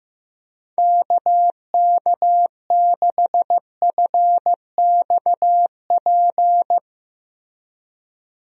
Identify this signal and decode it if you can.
Morse code, heard as a keyed tone, about 15 wpm, 700 Hz. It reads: KK6FXP
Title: USA Call Signs - 6 Characters - Repeated 2x - 15wpm